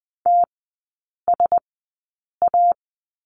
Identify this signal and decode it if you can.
Morse code, heard as a keyed tone, about 20 wpm, spaced as for 8 wpm, 700 Hz. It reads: TSA
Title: 100 Lesson 17e - Letter u - Three Characters 20wpm